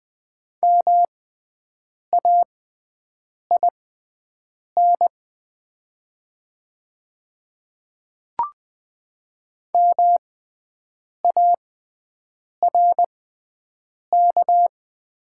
Morse code, heard as a keyed tone, about 20 wpm, spaced as for 7 wpm, 690 Hz. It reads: MAIN MARK
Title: 6x Character Spacing - Top 300 Words - 20wpm